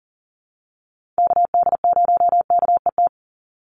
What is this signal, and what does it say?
Morse code, heard as a keyed tone, about 40 wpm, 700 Hz. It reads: KD0KET
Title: USA Call Signs 40wpm